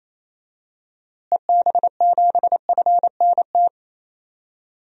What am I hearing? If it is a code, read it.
Morse code, heard as a keyed tone, about 27 wpm, 700 Hz. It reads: EB7FNT
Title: DXCC-100 Call Signs - 6 Characters - Repeated 2x - 28wpm